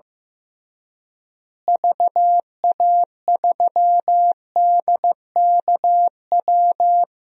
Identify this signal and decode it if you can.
Morse code, heard as a keyed tone, about 15 wpm, 700 Hz. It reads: VA3DKW
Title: DXCC-100 Call Signs - 6 Characters - Repeated 3x - 15wpm